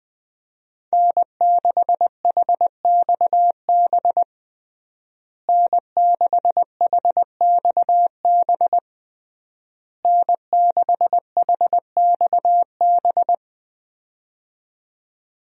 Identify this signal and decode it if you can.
Morse code, heard as a keyed tone, about 20 wpm, 700 Hz. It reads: N6HXB N6HXB N6HXB
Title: USA Call Signs - 5 Characters - Repeated 3x - 20wpm